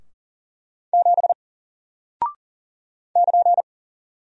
Morse code, heard as a keyed tone, about 40 wpm, 700 Hz. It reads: 7 <KN>
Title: Single Character (Letter:Number:Punctuation:Prosign) Alt 40wpm